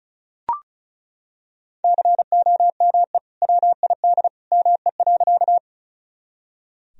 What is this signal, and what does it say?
Morse code, heard as a keyed tone, about 34 wpm, 700 Hz. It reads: COME WID ME.
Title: Sentences from Top 200 Words and Abbreviations 35wpm